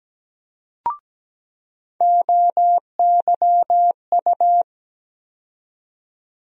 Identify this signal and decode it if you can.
Morse code, heard as a keyed tone, about 17 wpm, 700 Hz. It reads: OYU
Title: USA Call Sign Postfixes - 3 Letters 17wpm